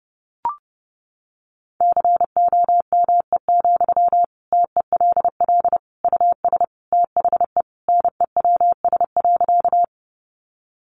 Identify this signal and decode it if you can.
Morse code, heard as a keyed tone, about 28 wpm, 700 Hz. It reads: COME, TELL US THE NEWS.
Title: Sentences from Top 800 Words 30wpm